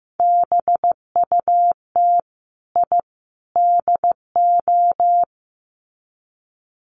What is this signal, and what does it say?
Morse code, heard as a keyed tone, about 15 wpm, 700 Hz. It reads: BUT I DO
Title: Sets of 3 Words 15wpm